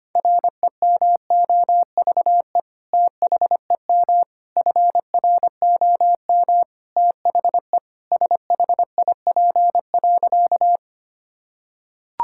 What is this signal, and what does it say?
Morse code, heard as a keyed tone, about 25 wpm, 700 Hz. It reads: REMOVE THEM FROM THE SHIP.
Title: Sentences from Top 1000 Words - Review Part 5 - 25wpm